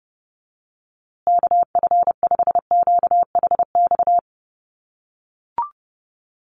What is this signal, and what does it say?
Morse code, heard as a keyed tone, about 30 wpm, 700 Hz. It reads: KF5QHX
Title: USA Call Signs Repeated 2x 30wpm